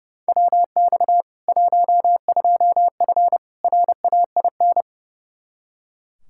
Morse code, heard as a keyed tone, about 30 wpm, 700 Hz. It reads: WX 12F RAIN
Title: K1USN SST QSO Practice 30wpm